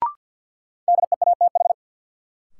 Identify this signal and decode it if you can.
Morse code, heard as a keyed tone, about 50 wpm, 700 Hz. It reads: DEATH
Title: Sentences from Top 1100 Words - New Word Drill - 50wpm